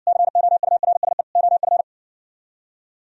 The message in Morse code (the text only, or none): COURSE OF